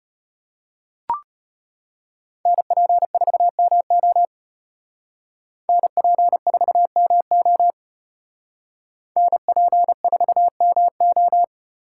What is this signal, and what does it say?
Morse code, heard as a keyed tone, about 33 wpm, 700 Hz. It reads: NP4MO NP4MO NP4MO